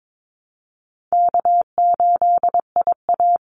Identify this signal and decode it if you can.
Morse code, heard as a keyed tone, about 22 wpm, 700 Hz. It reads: K8IA
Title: Easy Call Signs Repeated 3x - 22wpm